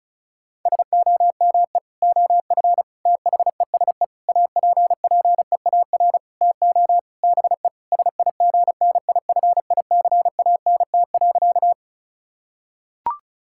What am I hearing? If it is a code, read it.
Morse code, heard as a keyed tone, about 36 wpm, 700 Hz. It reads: SOME OF THESE APPEAR TO BE SIGNIFICANT.
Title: Sentences from Top 1000 Words - Review Part 6 - 35wpm